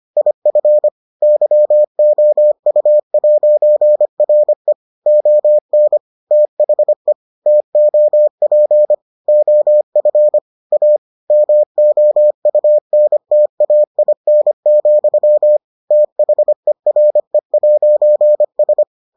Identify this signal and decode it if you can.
Morse code, heard as a keyed tone, about 25 wpm, 600 Hz. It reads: IF YOU'RE ON THE TOP OF A MOUNTAIN, THERE'S